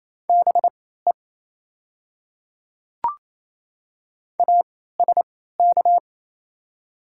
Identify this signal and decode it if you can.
Morse code, heard as a keyed tone, about 28 wpm, spaced as for 16 wpm, 700 Hz. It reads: BE ASK
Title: 3x Character Spacing - Top 200 Words - 28wpm